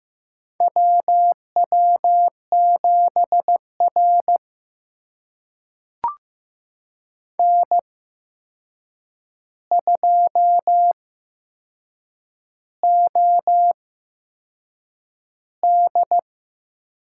Morse code, heard as a keyed tone, about 15 wpm, spaced as for 6 wpm, 700 Hz. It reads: WW7R N2OD
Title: USA Call Signs - 4 Characters - 8x Character Spacing - 15wpm